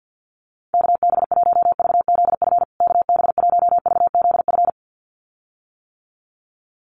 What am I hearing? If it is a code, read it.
Morse code, heard as a keyed tone, about 50 wpm, 700 Hz. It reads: KB1VZF KB1VZF
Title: USA Call Signs Repeated 2x 50wpm